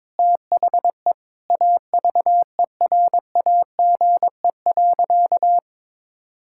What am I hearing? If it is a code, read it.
Morse code, heard as a keyed tone, about 21 wpm, 700 Hz. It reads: THE AVERAGE.
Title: Sentences from Top 1000 Words - Review Part 6 - 22wpm